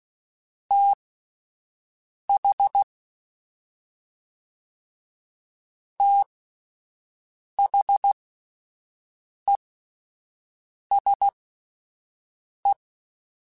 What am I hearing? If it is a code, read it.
Morse code, heard as a keyed tone, about 16 wpm, spaced as for 5 wpm, 800 Hz. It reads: TH THESE
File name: Combinations / Combinations-mp3s / Combinations@5